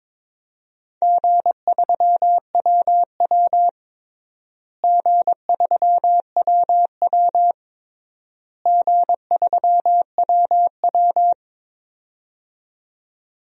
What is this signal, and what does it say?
Morse code, heard as a keyed tone, about 22 wpm, 700 Hz. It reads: G3WW G3WW G3WW
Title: DXCC-100 Call Signs - 4 Characters - Repeated 3x - 22wpm